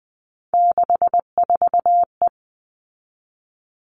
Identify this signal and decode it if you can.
Morse code, heard as a keyed tone, about 20 wpm, 700 Hz. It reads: 64E
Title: All Cut Numbers - 3-Digit 20wpm